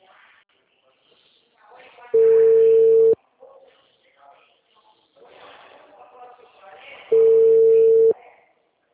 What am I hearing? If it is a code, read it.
Ringback tone (China, 450 Hz), with speech.
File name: out- -2007- - - .50945